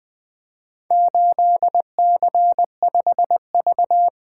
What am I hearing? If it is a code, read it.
Morse code, heard as a keyed tone, about 24 wpm, 700 Hz. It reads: 8C5V
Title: Speed-Racing - Four Letter-Number Combinations - 20wpm